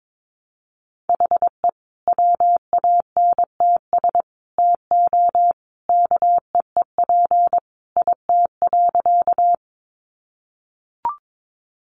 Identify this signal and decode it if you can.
Morse code, heard as a keyed tone, about 22 wpm, 700 Hz. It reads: HE WANTS TO KEEP IT.
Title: Additional Easy Phrases 22wpm